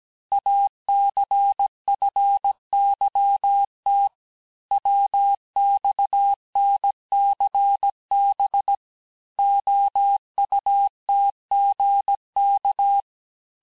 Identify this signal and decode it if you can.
Morse code, heard as a keyed tone, about 17 wpm, 800 Hz. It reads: ACFYT WXNCB OUTGK
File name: cours-du-lundi files / cours-1587-texte-chiffre-vitesse-16MPM